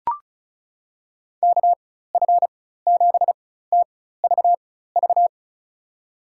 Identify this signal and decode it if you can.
Morse code, heard as a keyed tone, about 34 wpm, spaced as for 16 wpm, 700 Hz. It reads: KF7TVV
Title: USA Call Signs - 6 Characters - 4x Character Spacing - 35wpm